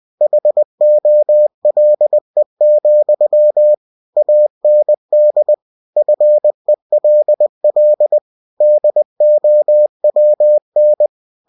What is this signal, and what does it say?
Morse code, heard as a keyed tone, about 20 wpm, 600 Hz. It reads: HOLE, AND FELL DOWN